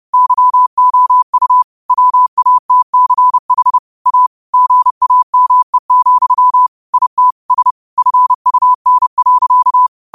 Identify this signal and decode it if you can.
Morse code, heard as a keyed tone, about 30 wpm, 1000 Hz. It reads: YOU WATCH A GAME, IT S FUN.
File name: qotdfiles / 30wpm